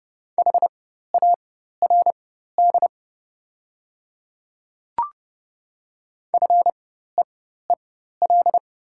Morse code, heard as a keyed tone, about 30 wpm, spaced as for 14 wpm, 690 Hz. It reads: HARD FEEL
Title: 4x Character Spacing - Top 500 Words - 30wpm